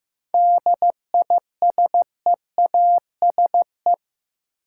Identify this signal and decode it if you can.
Morse code, heard as a keyed tone, about 15 wpm, 700 Hz. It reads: DISEASE
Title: Sentences from Top 1000 Words - Review Part 6 - New Word Drill - 15wpm